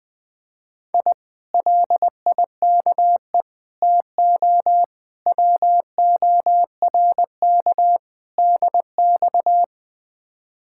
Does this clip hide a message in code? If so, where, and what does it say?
Yes: Morse code from 0.9 to 9.6 s, heard as a keyed tone, about 20 wpm, 700 Hz. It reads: I LIKE TO WORK DX